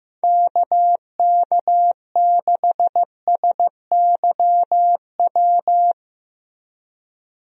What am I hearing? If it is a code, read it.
Morse code, heard as a keyed tone, about 15 wpm, 700 Hz. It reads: KK6SYW
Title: USA Call Signs - 6 Characters - Repeated 3x - 15wpm